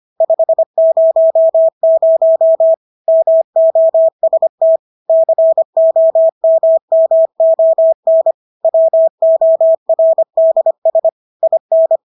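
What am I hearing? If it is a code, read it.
Morse code, heard as a keyed tone, about 25 wpm, 650 Hz. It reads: 500 MOST COMMON WORDS IN
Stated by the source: Morse practice of English words at 25 wpm from CW med Gnister.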